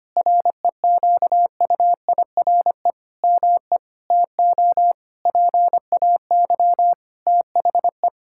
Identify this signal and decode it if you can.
Morse code, heard as a keyed tone, about 25 wpm, 700 Hz. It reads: REQUIRE ME TO PAY THE
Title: Sentences from Top 1000 Words - Review Part 6 - 25wpm